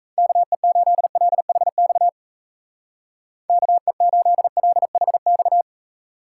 Morse code, heard as a keyed tone, about 34 wpm, 700 Hz. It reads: KE8LHX KE8LHX
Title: Speed Racing - USA Call Signs 28wpm